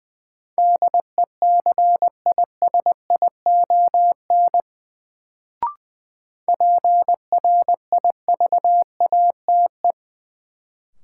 Morse code, heard as a keyed tone, about 20 wpm, 700 Hz. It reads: DECISION PRIVATE
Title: Sentences from Top 1000 Words - New Word Drill - 20wpm